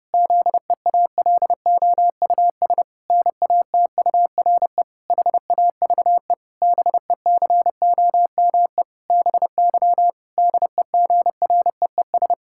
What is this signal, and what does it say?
Morse code, heard as a keyed tone, about 30 wpm, 700 Hz. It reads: ZEALOUS NATURE HAVE BECOME BY DEGREES
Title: Pangram Challenge v2 - 30wpm